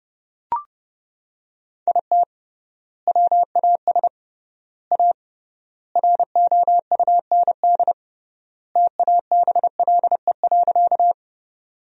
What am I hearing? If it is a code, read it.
Morse code, heard as a keyed tone, about 30 wpm, 700 Hz. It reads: IT WAS A ROUND TABLE.
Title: 3x Word-Spacing - Sentences from Top 400 Words - 30wpm